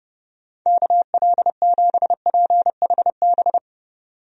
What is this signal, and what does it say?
Morse code, heard as a keyed tone, about 30 wpm, 700 Hz. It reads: KL7PHB
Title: DXCC-100 Call Signs - 6 Characters - 30wpm